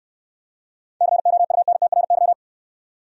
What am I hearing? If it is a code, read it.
Morse code, heard as a keyed tone, about 69 wpm, 690 Hz. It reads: COUNTRY